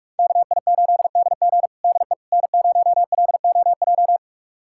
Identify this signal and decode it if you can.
Morse code, heard as a keyed tone, about 46 wpm, 700 Hz. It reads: KI8DG DE N0LOJ
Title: USA Callsign DE Callsign - 5 Characters - 45wpm